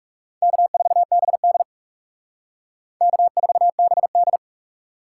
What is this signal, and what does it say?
Morse code, heard as a keyed tone, about 38 wpm, 700 Hz. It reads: K4BD K4BD